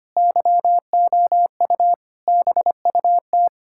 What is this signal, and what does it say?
Morse code, heard as a keyed tone, about 25 wpm, 700 Hz. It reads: YOU BUT
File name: practice / 25wpm / Sentences with 9-Word Phrases - 25wpm